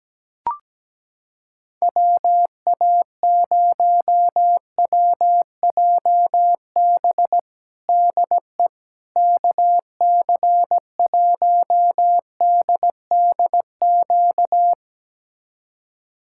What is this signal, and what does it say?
Morse code, heard as a keyed tone, about 17 wpm, 700 Hz. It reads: WA0WJB DE KC1DDQ
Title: USA Callsign DE Callsign - 6 Characters - 17wpm